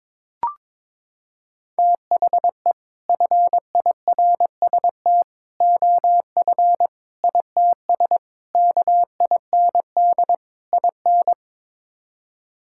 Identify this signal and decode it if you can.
Morse code, heard as a keyed tone, about 22 wpm, 700 Hz. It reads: THE FIRST OF ITS KIND IN
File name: practice / 22wpm / Sets of 6 Words - 22wpm